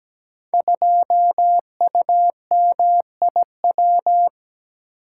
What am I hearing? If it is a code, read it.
Morse code, heard as a keyed tone, about 17 wpm, 700 Hz. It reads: 2UMIW
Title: Code Groups - 5 Letter:Numbers - 17wpm